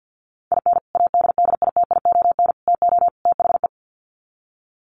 Morse code, heard as a keyed tone, about 50 wpm, 700 Hz. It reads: IN ADDITION TO THE
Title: Sets of 4 Words 50wpm